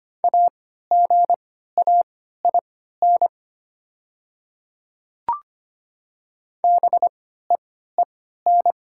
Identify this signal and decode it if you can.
Morse code, heard as a keyed tone, about 25 wpm, spaced as for 14 wpm, 700 Hz. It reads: AGAIN BEEN
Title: 3x Character Spacing - Top 200 Words - 25wpm